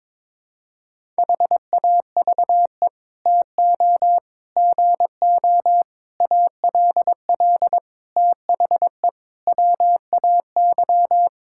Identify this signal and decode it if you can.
Morse code, heard as a keyed tone, about 22 wpm, 700 Hz. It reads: HAVE TO GO ALL THE WAY
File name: practice / 22wpm / Sets of 8 Words - 22wpm